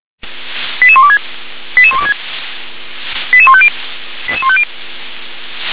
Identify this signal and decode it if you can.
Five-tone selective call (ZVEI), heard as quick runs of five tones, tones of about 70 ms. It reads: 88126 88126 88150 88150